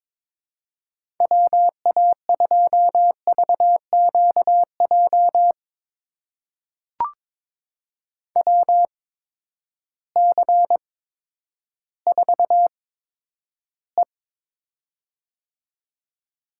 Morse code, heard as a keyed tone, about 22 wpm, spaced as for 11 wpm, 700 Hz. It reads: WA2VQJ WC4E
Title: USA Call Signs - 8x Character Spacing - 22wpm